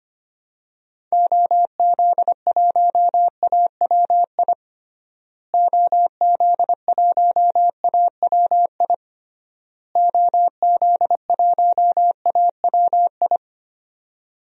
Morse code, heard as a keyed tone, about 25 wpm, 690 Hz. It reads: OZ1AWI OZ1AWI OZ1AWI
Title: DXCC-100 Call Signs - 6 Characters - Repeated 3x - 25wpm